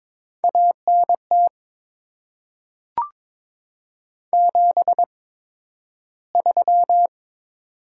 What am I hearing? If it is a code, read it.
Morse code, heard as a keyed tone, about 22 wpm, spaced as for 12 wpm, 700 Hz. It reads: ANT 73
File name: Top 20 QSO Elements - 8x Character Spacing - 22wpm